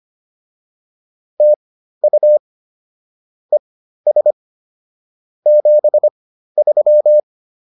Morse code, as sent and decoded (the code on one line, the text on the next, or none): - ..- / . ... / --... ...--
TU ES 73